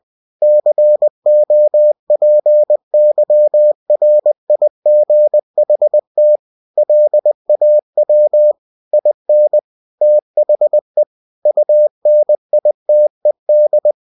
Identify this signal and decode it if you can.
Morse code, heard as a keyed tone, about 20 wpm, 600 Hz. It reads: COPYRIGHT LAW IN THE UNITED
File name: cw-books / winnie-the-pooh / winnie-20wpm-page-0114